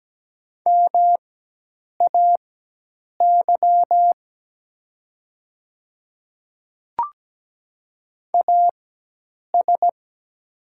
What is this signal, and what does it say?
Morse code, heard as a keyed tone, about 17 wpm, spaced as for 8 wpm, 700 Hz. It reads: MAY AS
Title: 4x Character Spacing - Top 100 Words - 17wpm